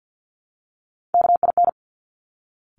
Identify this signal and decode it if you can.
Morse code, heard as a keyed tone, about 50 wpm, 700 Hz. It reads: KIN
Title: Top Three Letter Combinations in Words 50wpm